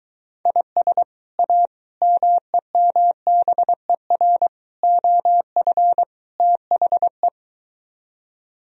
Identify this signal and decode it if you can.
Morse code, heard as a keyed tone, about 23 wpm, 700 Hz. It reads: IS A MEMBER OF THE